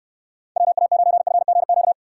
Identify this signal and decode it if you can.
Morse code, heard as a keyed tone, about 69 wpm, 700 Hz. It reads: WA9FGQ